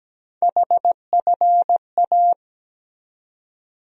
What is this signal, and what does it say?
Morse code, heard as a keyed tone, about 17 wpm, 700 Hz. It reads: HFA
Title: USA Call Sign Postfixes - 3 Letters 17wpm